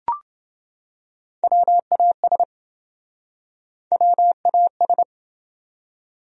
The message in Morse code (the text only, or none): WAS WAS